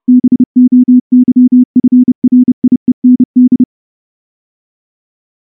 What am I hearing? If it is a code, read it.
Morse code, heard as a keyed tone, about 30 wpm, 250 Hz. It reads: BOYFRIEND
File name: morse / morse-training-30WPM